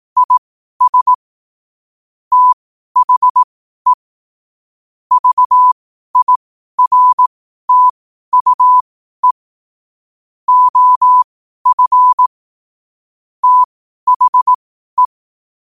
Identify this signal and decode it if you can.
Morse code, heard as a keyed tone, about 18 wpm, spaced as for 12 wpm, 1000 Hz. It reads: IS THE VIRTUE OF THE
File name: qotdfiles / 12wpm